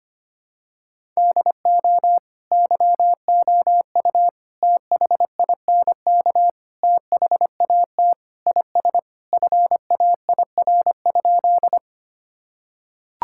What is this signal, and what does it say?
Morse code, heard as a keyed tone, about 25 wpm, 700 Hz. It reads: DO YOU THINK THAT IS FAIR?
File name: practice / 25wpm / Sentences with 4-Word Phrases - 25wpm